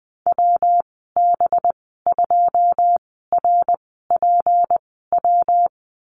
Morse code, heard as a keyed tone, about 19 wpm, spaced as for 14 wpm, 700 Hz. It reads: WB2RPW
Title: 213 Lesson 34g - Number 8 - Call Signs 20wpm